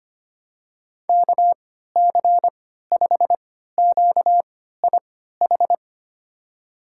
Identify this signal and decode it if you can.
Morse code, heard as a keyed tone, about 25 wpm, spaced as for 14 wpm, 700 Hz. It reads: KC5QIH
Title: USA Call Signs - 6 Characters - 3x Character Spacing - 25wpm